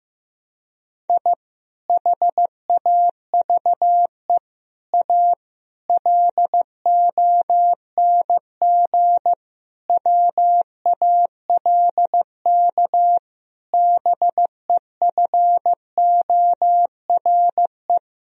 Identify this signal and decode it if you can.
Morse code, heard as a keyed tone, about 15 wpm, 700 Hz. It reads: I HAVE A LONG WALK BEFORE
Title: Sentences from Top 1000 Words - Review Part 1 - 15wpm